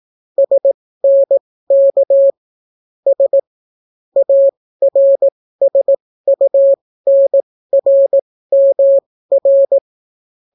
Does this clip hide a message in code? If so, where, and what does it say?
Yes: Morse code from 0.4 to 9.8 s, heard as a keyed tone, about 18 wpm, spaced as for 14 wpm, 550 Hz. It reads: SNK S ARSUNRMR